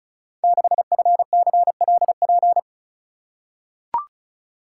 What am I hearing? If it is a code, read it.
Morse code, heard as a keyed tone, about 36 wpm, 700 Hz. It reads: 6FCLP
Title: Code Groups - 5 Letter:Numbers - 35wpm